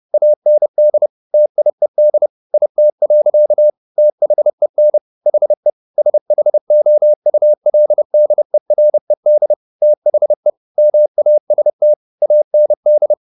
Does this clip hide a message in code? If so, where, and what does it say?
Yes: Morse code from 0.1 to 13.1 s, heard as a keyed tone, about 30 wpm, 600 Hz. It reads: AND TIED IT. THEN HE SHOULDERED THE MAST AND